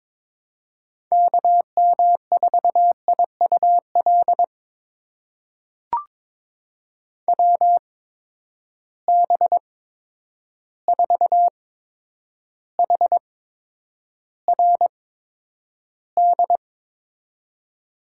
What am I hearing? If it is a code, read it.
Morse code, heard as a keyed tone, about 22 wpm, spaced as for 9 wpm, 700 Hz. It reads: KM4IUL WB4HRD
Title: USA Call Signs - 6 Characters - 8x Character Spacing - 22wpm